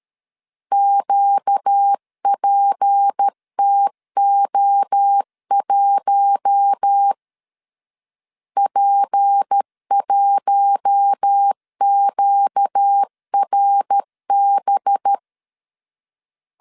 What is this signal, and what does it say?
Morse code, heard as a keyed tone, about 13 wpm, 800 Hz. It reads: QPTO1 P1QRB
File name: LEZIONI / 6 PKX0 / 6 P 12wpm 2033